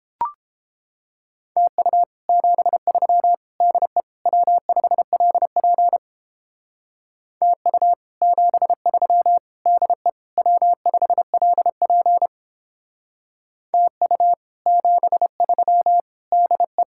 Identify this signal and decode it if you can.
Morse code, heard as a keyed tone, about 31 wpm, 700 Hz. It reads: TU 73 DE W5LP TU 73 DE W5LP TU 73 DE